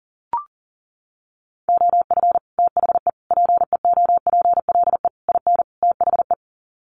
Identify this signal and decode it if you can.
Morse code, heard as a keyed tone, about 40 wpm, 700 Hz. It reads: OF THE PEOPLE IN THE
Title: Sets of 5 Words Encore 40wpm